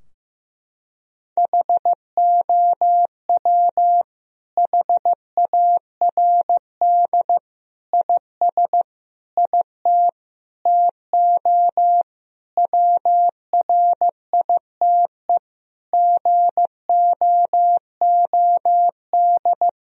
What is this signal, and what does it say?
Morse code, heard as a keyed tone, about 15 wpm, 700 Hz. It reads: HOW HARD IS IT TO WRITE GOOD
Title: Sentences from Top 1000 Words - Review Part 3 - 15wpm